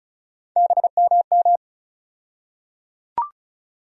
Morse code, heard as a keyed tone, about 34 wpm, 700 Hz. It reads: BMM
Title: USA Call Sign Postfixes - 3 Letters 35wpm